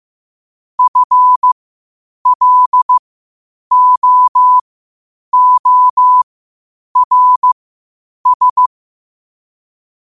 Morse code, heard as a keyed tone, about 15 wpm, spaced as for 8 wpm, 1000 Hz. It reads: FLOORS
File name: qotdfiles / 8wpm